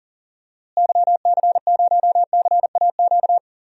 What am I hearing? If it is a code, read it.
Morse code, heard as a keyed tone, about 38 wpm, 700 Hz. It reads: YC0CAQ
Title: DXCC-100 Call Signs - 6 Characters - Repeated 2x - 40wpm